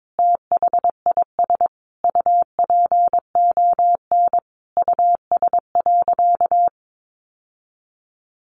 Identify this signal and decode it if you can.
Morse code, heard as a keyed tone, about 22 wpm, 700 Hz. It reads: THIS UPON US.
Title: Sentences from Top 1400 Words 22wpm